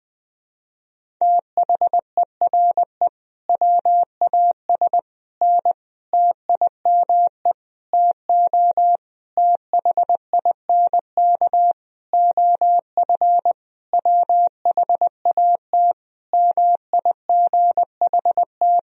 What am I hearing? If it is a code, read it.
Morse code, heard as a keyed tone, about 20 wpm, 700 Hz. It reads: THERE WAS N TIME TO THINK OF WHAT MIGHT
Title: Sentences from Top 400 Words and Abbreviations 20wpm